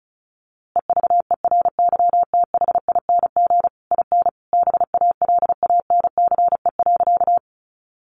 Morse code, heard as a keyed tone, about 34 wpm, 700 Hz. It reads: EVERYTHING IN BALANCE.
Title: A Sound Of Thunder 35wpm